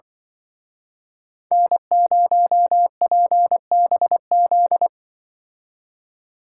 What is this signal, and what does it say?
Morse code, heard as a keyed tone, about 24 wpm, 700 Hz. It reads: N0PBZ
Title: Speed Racing - USA 1x3 Call Signs 20wpm